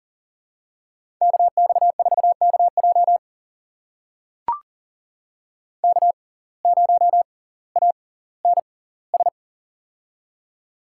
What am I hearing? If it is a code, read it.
Morse code, heard as a keyed tone, about 40 wpm, spaced as for 20 wpm, 700 Hz. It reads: KX4KJ K0ANS
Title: USA Call Signs - 5 Characters - 6x Character Spacing - 40wpm